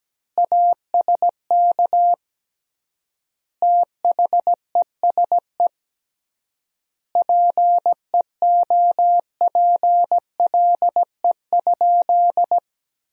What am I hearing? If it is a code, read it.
Morse code, heard as a keyed tone, about 17 wpm, 700 Hz. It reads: ASK THESE PEOPLE?
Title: 3x Word-Spacing - Sentences from Top 200 Words - 17wpm